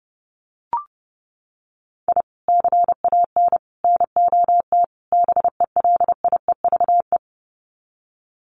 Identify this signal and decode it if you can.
Morse code, heard as a keyed tone, about 30 wpm, 700 Hz. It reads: I CAN NOT BELIEVE
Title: Sets of 4 Words 30wpm